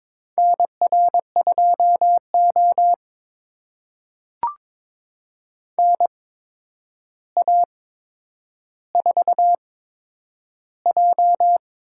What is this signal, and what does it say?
Morse code, heard as a keyed tone, about 22 wpm, spaced as for 9 wpm, 700 Hz. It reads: NR2O NA4J
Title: USA Call Signs - 4 Characters - 8x Character Spacing - 22wpm